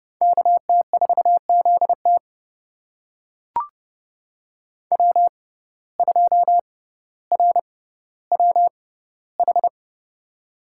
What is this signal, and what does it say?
Morse code, heard as a keyed tone, about 30 wpm, spaced as for 15 wpm, 700 Hz. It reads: KT4ZT W2RWH
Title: USA Call Signs - 5 Characters - 6x Character Spacing - 30wpm